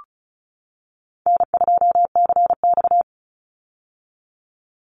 Morse code, heard as a keyed tone, about 34 wpm, 700 Hz. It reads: N2CX
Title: Call Signs - 35wpm